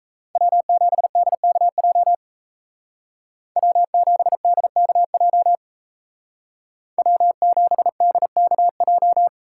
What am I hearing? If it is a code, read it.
Morse code, heard as a keyed tone, about 33 wpm, 700 Hz. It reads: W7DKJ W7DKJ W7DKJ
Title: Speed Racing - USA 1x3 Call Signs 28wpm